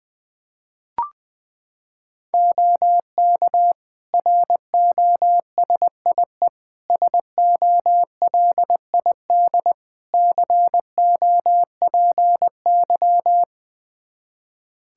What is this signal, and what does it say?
Morse code, heard as a keyed tone, about 20 wpm, 700 Hz. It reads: OK ROSIE SOLID COPY
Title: Practice QSO - Exchange 3 - More Info - 20wpm